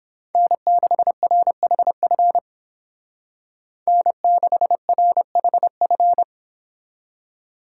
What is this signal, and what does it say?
Morse code, heard as a keyed tone, about 29 wpm, 700 Hz. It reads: N6RHF N6RHF